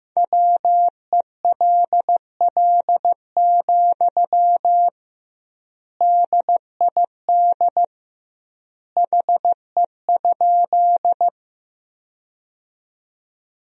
Morse code, heard as a keyed tone, about 15 wpm, 700 Hz. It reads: WELL, DID HE?
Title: 2x Word-Spacing - Sentences from Top 200 Words - 15wpm